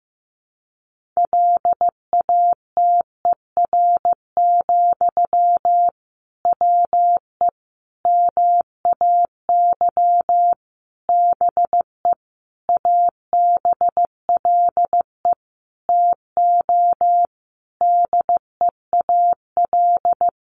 Morse code, heard as a keyed tone, about 15 wpm, 700 Hz. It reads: LATER, WE MAY BE ABLE TO DEAL
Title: Sentences from Top 800 Words 15wpm